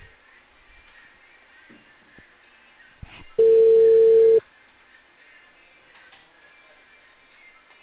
Ringback tone (China, 450 Hz).